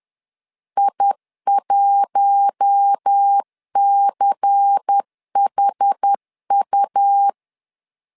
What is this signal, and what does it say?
Morse code, heard as a keyed tone, about 10 wpm, 800 Hz. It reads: I1CHU
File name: LEZIONI / 5 LBVC / 5 LBVC 10wpm 2349